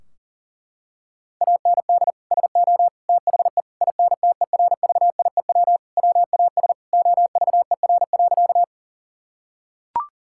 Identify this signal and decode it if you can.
Morse code, heard as a keyed tone, about 40 wpm, 700 Hz. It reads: AND SO THE INTERVIEW WAS OVER.
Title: Sentences from Top 1600 Words 40wpm